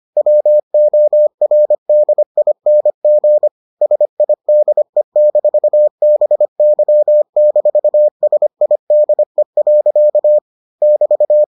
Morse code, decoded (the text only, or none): WORDING SIDE-BY-SIDE. <BT>